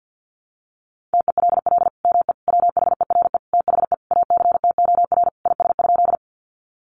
Morse code, heard as a keyed tone, about 50 wpm, 700 Hz. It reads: TELL ME WHERE THE ACTOR IS?
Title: Sentences from Top 1500 Words 50wpm